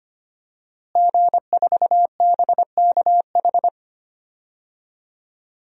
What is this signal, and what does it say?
Morse code, heard as a keyed tone, about 25 wpm, 700 Hz. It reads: G4BKH
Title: DXCC-100 Call Signs - 5 Characters - Repeated 3x - 25wpm